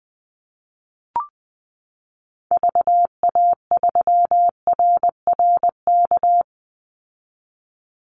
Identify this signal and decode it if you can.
Morse code, heard as a keyed tone, about 19 wpm, 700 Hz. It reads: VA3RRK
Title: Call Signs - 20wpm